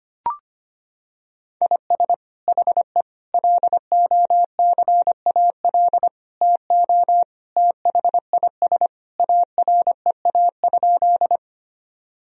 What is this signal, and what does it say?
Morse code, heard as a keyed tone, about 25 wpm, 700 Hz. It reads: IS HE LOCAL TO THIS AREA?
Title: Sentences from Top 1000 Words - Review Part 3 - 25wpm